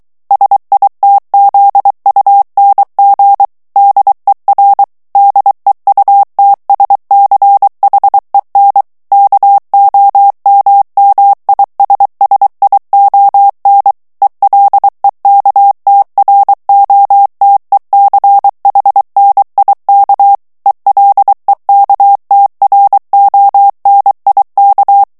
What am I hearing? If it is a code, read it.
Morse code, heard as a keyed tone, about 23 wpm, 800 Hz. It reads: SITZUNG DER DEUTSCHEN KOMMISSION ELEKTROTECHNIK ELEKTRONIK
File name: ALLEMAND texte clair 1 vitesse 22 mots minute